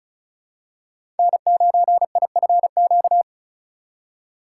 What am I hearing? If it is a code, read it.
Morse code, heard as a keyed tone, about 36 wpm, 700 Hz. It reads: N9IFQ
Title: USA Call Signs - 3x Character Spacing - 35wpm